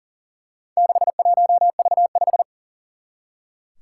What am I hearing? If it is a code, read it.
Morse code, heard as a keyed tone, about 40 wpm, 700 Hz. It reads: 61V5
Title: Four Challenging Letter-Number Combinations 40wpm